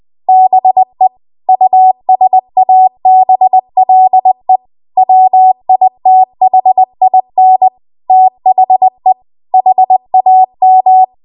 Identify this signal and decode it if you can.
Morse code, heard as a keyed tone, about 20 wpm, 750 Hz. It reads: BE USABLE WITHIN THE HAM